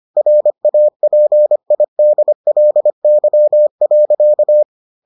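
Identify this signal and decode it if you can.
Morse code, heard as a keyed tone, about 25 wpm, 600 Hz. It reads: RAPIDLY.
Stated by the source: Morse edition of the Foundations of Amateur Radio episodes.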